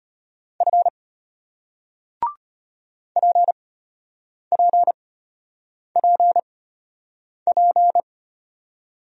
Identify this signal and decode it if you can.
Morse code, heard as a keyed tone, about 33 wpm, spaced as for 8 wpm, 700 Hz. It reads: F PPPP